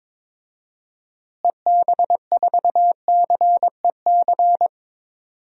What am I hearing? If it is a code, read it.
Morse code, heard as a keyed tone, about 22 wpm, 700 Hz. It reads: EB4CEC
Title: DXCC-100 Call Signs - 6 Characters - Repeated 3x - 22wpm